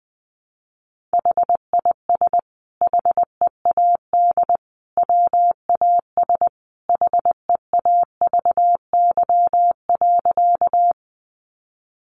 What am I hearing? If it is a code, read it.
Morse code, heard as a keyed tone, about 20 wpm, 700 Hz. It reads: HIS HEAD WAS HEAVY.
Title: Sentences from Top 1100 Words 20wpm